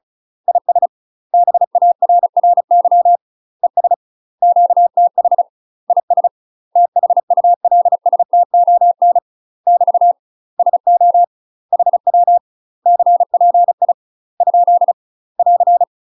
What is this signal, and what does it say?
Morse code, heard as a keyed tone, about 36 wpm, 700 Hz. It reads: IS BARRY ES QTH IS THULSTON <BT> SO HW CPI ? <AR>
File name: UK MASTER QSOs TRAINING/35WPM